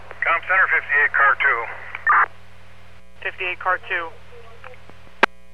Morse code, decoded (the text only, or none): none